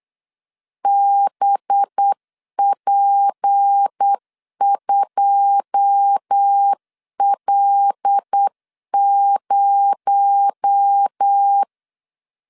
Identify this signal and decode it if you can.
Morse code, heard as a keyed tone, about 8 wpm, 800 Hz. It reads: BP2L0
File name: LEZIONI / 9 286 8wpm 3425